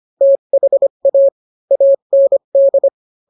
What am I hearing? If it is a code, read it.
Morse code, heard as a keyed tone, about 25 wpm, 550 Hz. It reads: THA AND